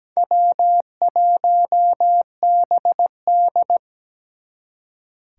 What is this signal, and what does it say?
Morse code, heard as a keyed tone, about 20 wpm, 700 Hz. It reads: W1BD
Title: Speed Racing - USA 1x2 Call Signs 17wpm